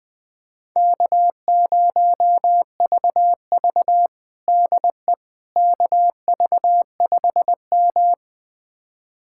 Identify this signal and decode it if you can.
Morse code, heard as a keyed tone, about 20 wpm, 700 Hz. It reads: K0VV DE KV5M
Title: USA Callsign DE Callsign - 4 Characters - 20wpm